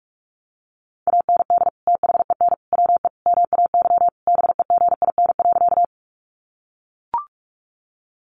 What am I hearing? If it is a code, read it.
Morse code, heard as a keyed tone, about 44 wpm, 700 Hz. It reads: AND THEN WE MAY BEGIN.
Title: Sentences from Top 300 Words 45wpm